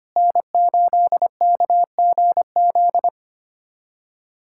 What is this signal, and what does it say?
Morse code, heard as a keyed tone, about 25 wpm, 700 Hz. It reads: N8KGZ
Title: USA Callsign DE Callsign - 5 Characters - 25wpm